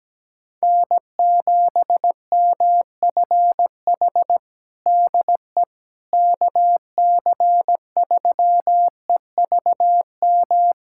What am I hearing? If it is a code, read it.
Morse code, heard as a keyed tone, about 17 wpm, 700 Hz. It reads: N7MFH DE KC3EVM
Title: USA Callsign DE Callsign 17wpm